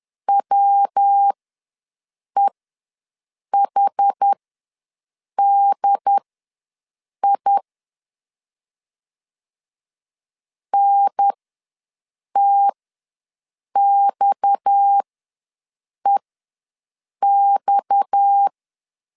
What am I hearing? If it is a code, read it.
Morse code, heard as a keyed tone, about 10 wpm, spaced as for 6 wpm, 800 Hz. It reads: WEHDI NTXEX
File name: LEZIONI / 6 PKX0 / 6 PKX ws 10wpm 3706